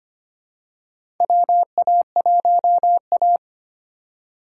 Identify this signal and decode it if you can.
Morse code, heard as a keyed tone, about 25 wpm, 700 Hz. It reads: WA1A
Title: USA Call Signs - 4 Characters - 3x Character Spacing - 25wpm